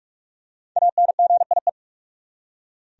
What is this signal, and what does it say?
Morse code, heard as a keyed tone, about 47 wpm, 700 Hz. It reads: ANGIE
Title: USA Common Names - Max 5 Letters - 45wpm